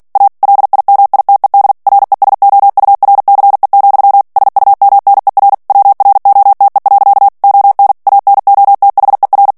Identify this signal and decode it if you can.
Morse code, heard as a keyed tone, about 48 wpm, 800 Hz. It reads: A LIMITED RESOURCE, SUMNER WROTE. ON ANOTHER
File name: ANGLAIS texte clair 3 vitesse 45 mots minute